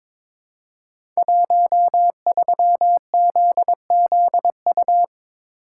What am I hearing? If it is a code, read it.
Morse code, heard as a keyed tone, about 22 wpm, 700 Hz. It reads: 13ZZU
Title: Code Groups - 5 Letter:Numbers - 22wpm